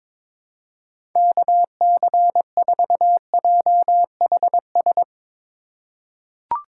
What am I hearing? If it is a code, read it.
Morse code, heard as a keyed tone, about 22 wpm, 700 Hz. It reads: KC4JHS
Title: USA Call Signs - 6 Characters - 4x Character Spacing - 22wpm